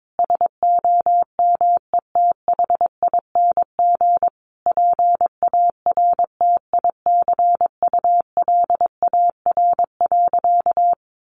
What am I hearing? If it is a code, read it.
Morse code, heard as a keyed tone, about 22 wpm, 700 Hz. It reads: SOMETHING PARTICULAR.
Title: Sentences from Top 600 Words 22wpm